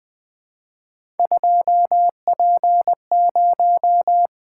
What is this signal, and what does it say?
Morse code, heard as a keyed tone, about 20 wpm, 700 Hz. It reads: 2P0
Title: Three Letter-Number Combinations - 20wpm